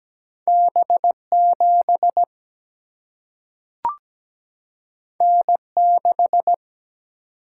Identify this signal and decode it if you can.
Morse code, heard as a keyed tone, about 17 wpm, 700 Hz. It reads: B7 N6
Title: Single-Letter Single-Number 17wpm